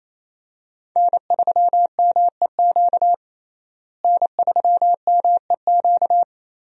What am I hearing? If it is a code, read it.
Morse code, heard as a keyed tone, about 28 wpm, 700 Hz. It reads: N3MEQ N3MEQ
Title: USA Call Signs - 5 Characters - Repeated 2x - 28wpm